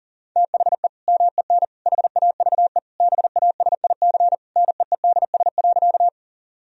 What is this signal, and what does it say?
Morse code, heard as a keyed tone, about 40 wpm, 700 Hz. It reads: THE MEN HAVE BASIC NEEDS.
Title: Sentences from Top 1000 Words - Review Part 3 - 40wpm